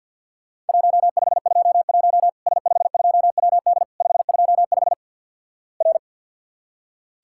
Morse code, heard as a keyed tone, about 48 wpm, 700 Hz. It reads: 1521 S52WD 525 R
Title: Worked All Europe DX Contest - QTC Exchange - 50wpm